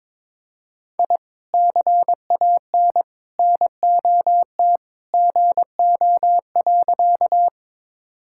Morse code, heard as a keyed tone, about 22 wpm, 700 Hz. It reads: I CAN NOT GO.
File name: practice / 22wpm / Sentences with 2-Word Phrases - 22wpm